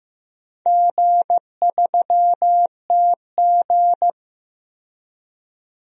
Morse code, heard as a keyed tone, about 15 wpm, 700 Hz. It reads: G3TG
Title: DXCC-100 Call Signs - 4 Characters - 15wpm